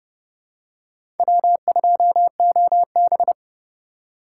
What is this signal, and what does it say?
Morse code, heard as a keyed tone, about 29 wpm, 700 Hz. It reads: W2OB
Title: USA Call Signs - 4 Characters - 6x Character Spacing - 30wpm